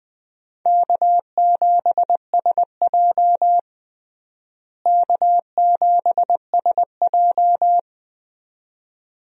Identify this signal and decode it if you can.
Morse code, heard as a keyed tone, about 20 wpm, 700 Hz. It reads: K7SJ K7SJ
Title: USA Call Signs - 4 Characters - Repeated 2x - 20wpm